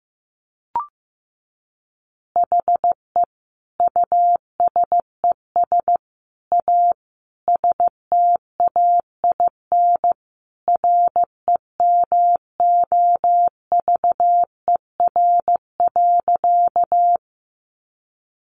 Morse code, heard as a keyed tone, about 15 wpm, 700 Hz. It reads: HE USES A STAIN REMOVER.
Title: Additional Easy Phrases 15wpm